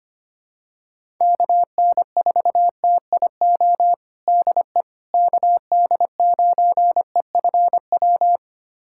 Morse code, heard as a keyed tone, about 25 wpm, 700 Hz. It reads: KN4TIO DE KD9EFW
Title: USA Callsign DE Callsign - 6 Characters - 25wpm